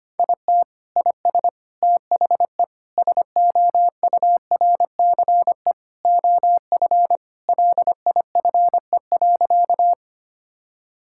Morse code, heard as a keyed tone, about 25 wpm, 700 Hz. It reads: IT IS THE SOURCE OF LIFE.
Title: Sentences from Top 1000 Words - Review Part 5 - 25wpm